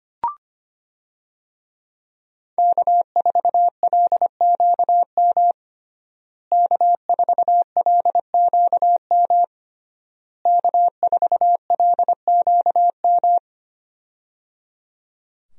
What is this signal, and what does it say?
Morse code, heard as a keyed tone, about 25 wpm, 700 Hz. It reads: K4LQM K4LQM K4LQM
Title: USA Call Signs - 5 Characters - Repeated 3x - 25wpm